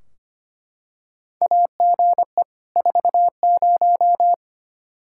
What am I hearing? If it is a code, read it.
Morse code, heard as a keyed tone, about 25 wpm, 700 Hz. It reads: AGE 40
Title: Practice QSO - Exchange 3 - More Info - 25wpm